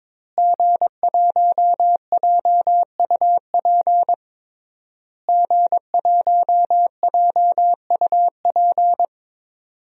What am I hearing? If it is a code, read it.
Morse code, heard as a keyed tone, about 22 wpm, 700 Hz. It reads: G1JUP G1JUP
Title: DXCC-100 Call Signs - 5 Characters - Repeated 3x - 22wpm